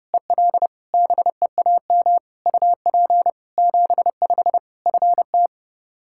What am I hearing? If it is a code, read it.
Morse code, heard as a keyed tone, about 30 wpm, 700 Hz. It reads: EL BEAM UP 75 FT
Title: K1USN SST QSO Practice 30wpm